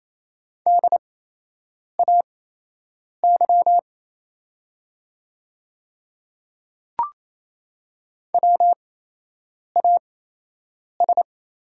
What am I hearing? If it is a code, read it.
Morse code, heard as a keyed tone, about 28 wpm, spaced as for 8 wpm, 700 Hz. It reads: DAY WAS
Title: 8x Character Spacing - Top 100 Words - 28wpm